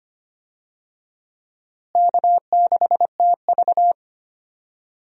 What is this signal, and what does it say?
Morse code, heard as a keyed tone, about 25 wpm, 700 Hz. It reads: K6TV
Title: USA Call Signs - 4 Characters - Repeated 3x - 25wpm